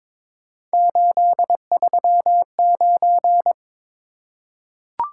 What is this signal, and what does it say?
Morse code, heard as a keyed tone, about 22 wpm, 700 Hz. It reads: 839
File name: practice / 22wpm / Three Letter-Number Combinations - 22wpm